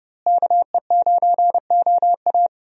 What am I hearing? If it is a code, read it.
Morse code, heard as a keyed tone, about 30 wpm, 700 Hz. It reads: KE9OA
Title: USA Call Signs - 5 Characters - Repeated 3x - 30wpm